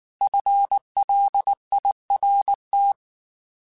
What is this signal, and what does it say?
Morse code, heard as a keyed tone, about 19 wpm, 800 Hz. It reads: FLIRT